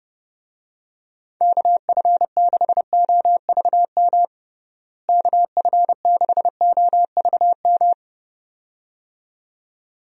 Morse code, heard as a keyed tone, about 30 wpm, 700 Hz. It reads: KF6OVM KF6OVM
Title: USA Call Signs - 6 Characters - Repeated 2x - 30wpm